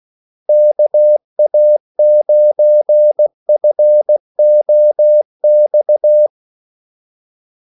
Morse code, heard as a keyed tone, about 16 wpm, 600 Hz. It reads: KA9FOX
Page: Kaldesignaler 16 wpm